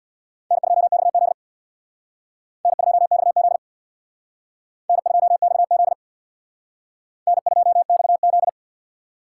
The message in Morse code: N2XZ N2XZ N2XZ N2XZ

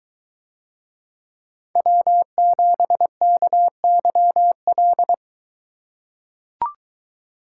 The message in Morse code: W7KYL

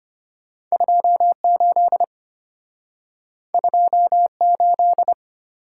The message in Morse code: 28 28

